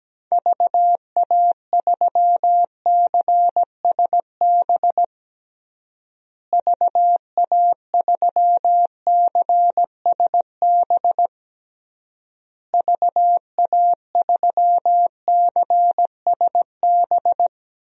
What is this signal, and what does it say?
Morse code, heard as a keyed tone, about 17 wpm, 700 Hz. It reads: VA3CSB VA3CSB VA3CSB
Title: DXCC-100 Call Signs - 6 Characters - Repeated 3x - 17wpm